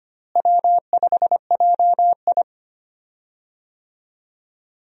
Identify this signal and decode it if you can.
Morse code, heard as a keyed tone, about 25 wpm, 700 Hz. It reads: W5JI
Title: USA Call Signs - 4 Characters - Repeated 3x - 25wpm